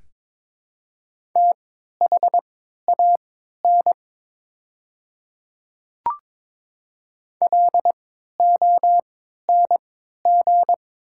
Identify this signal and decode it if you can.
Morse code, heard as a keyed tone, about 23 wpm, spaced as for 13 wpm, 700 Hz. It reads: THAN LONG
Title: 3x Character Spacing - Top 100 Words - 22wpm